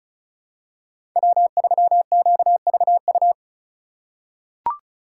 Morse code, heard as a keyed tone, about 36 wpm, 700 Hz. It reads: W3QVU
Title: USA Call Signs - 5 Characters - 3x Character Spacing - 35wpm